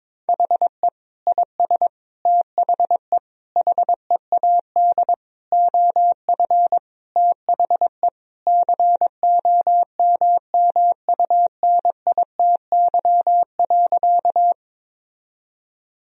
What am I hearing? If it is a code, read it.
Morse code, heard as a keyed tone, about 22 wpm, 700 Hz. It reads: HE IS THE HEAD OF THE COMMUNITY.
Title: Sentences from Top 1000 Words - Review Part 4 - 22wpm